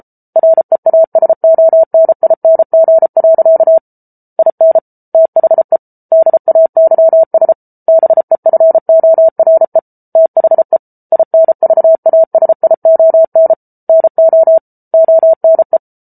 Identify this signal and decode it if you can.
Morse code, heard as a keyed tone, about 34 wpm, 650 Hz. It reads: REASONING. IN THE DAYS BEFORE THE INVASION NO ONE